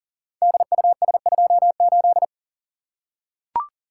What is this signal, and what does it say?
Morse code, heard as a keyed tone, about 39 wpm, 700 Hz. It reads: DUS28
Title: Code Groups - 5 Letter:Numbers - 40wpm